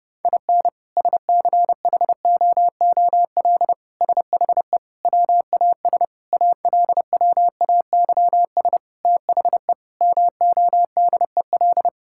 Morse code, heard as a keyed tone, about 30 wpm, 700 Hz. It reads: IN SCHOOL SHE WAS ALWAYS THE MODEL
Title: Sentences from Top 1000 Words - Review Part 5 - 30wpm